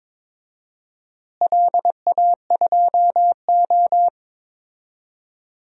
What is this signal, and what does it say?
Morse code, heard as a keyed tone, about 22 wpm, 700 Hz. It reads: LA2O
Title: DXCC-100 Call Signs - 4 Characters - Repeated 3x - 22wpm